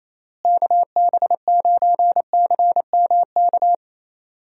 Morse code, heard as a keyed tone, about 28 wpm, 700 Hz. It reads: KB9CMK
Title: USA Call Signs - 6 Characters - Repeated 2x - 28wpm